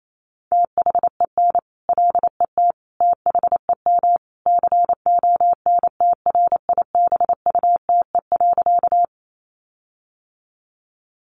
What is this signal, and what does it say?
Morse code, heard as a keyed tone, about 28 wpm, 700 Hz. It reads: THEN LET THEM CONTRIBUTE.
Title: Sentences from Top 1400 Words 28wpm